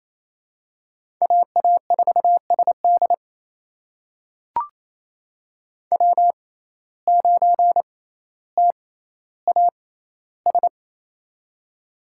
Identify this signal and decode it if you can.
Morse code, heard as a keyed tone, about 28 wpm, spaced as for 14 wpm, 700 Hz. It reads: AA4SD W9TAS
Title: USA Call Signs - 5 Characters - 6x Character Spacing - 28wpm